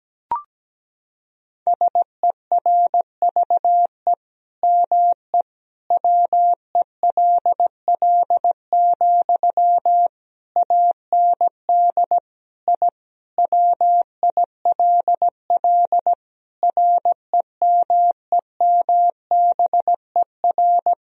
Morse code, heard as a keyed tone, about 17 wpm, 700 Hz. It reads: SERVE ME WELL, AND I WILL REMEMBER
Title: Sentences from Top 1000 Words - Review Part 2 - 17wpm